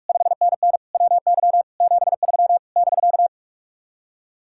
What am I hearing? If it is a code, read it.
Morse code, heard as a keyed tone, about 59 wpm, spaced as for 52 wpm, 700 Hz. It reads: 5NN WY 73 <BK>
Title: Speed-Racing Common QSO Phrases - 45wpm